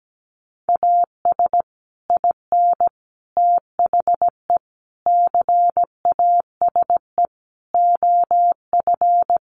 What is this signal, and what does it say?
Morse code, heard as a keyed tone, about 17 wpm, 700 Hz. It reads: AS IN THE CASE OF
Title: Sets of 5 Words 17wpm